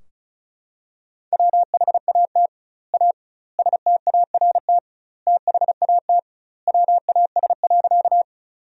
Morse code, heard as a keyed tone, about 36 wpm, 700 Hz. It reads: WHAT A START THAT WAS.
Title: 2x Word-Spacing - Sentences from Top 300 Words - 35wpm